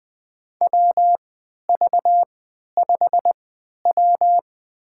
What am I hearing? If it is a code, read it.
Morse code, heard as a keyed tone, about 20 wpm, spaced as for 11 wpm, 700 Hz. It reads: WV5W
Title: USA Call Signs - 3x Character Spacing - 20wpm